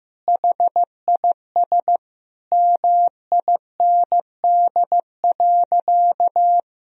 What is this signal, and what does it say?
Morse code, heard as a keyed tone, about 15 wpm, 700 Hz. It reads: HIS MIND.
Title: Sentences from Top 1000 Words - Review Part 2 - 15wpm